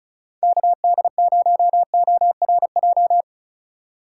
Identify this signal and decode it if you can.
Morse code, heard as a keyed tone, about 34 wpm, 690 Hz. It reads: KD0ORJ
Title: USA Call Signs - 6 Characters - 2x Character Spacing - 35wpm